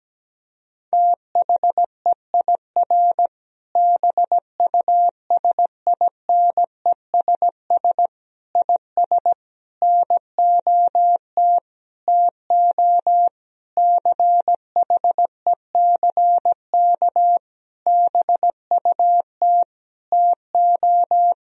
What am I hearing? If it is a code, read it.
Morse code, heard as a keyed tone, about 17 wpm, 700 Hz. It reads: THEIR BUSINESS IS NOT TO CHECK BUT TO
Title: Sentences from Top 1000 Words - Review Part 3 - 17wpm